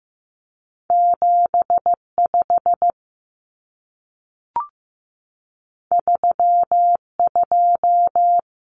Morse code, heard as a keyed tone, about 15 wpm, 700 Hz. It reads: 75 32
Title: Numbers - 2-Digit 15wpm